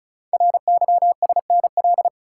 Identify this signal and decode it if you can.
Morse code, heard as a keyed tone, about 36 wpm, 700 Hz. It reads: RYSNL
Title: Code Groups - 5 Letters - 35wpm